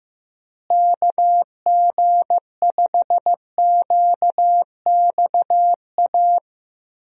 Morse code, heard as a keyed tone, about 15 wpm, 700 Hz. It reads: KG5QXA
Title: USA Call Signs - 6 Characters - Repeated 3x - 15wpm